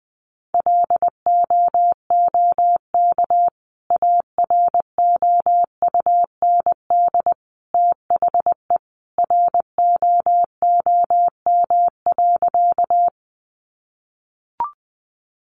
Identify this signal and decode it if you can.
Morse code, heard as a keyed tone, about 20 wpm, 700 Hz. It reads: LOOK AROUND THE ROOM.
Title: Sentences from Top 600 Words 20wpm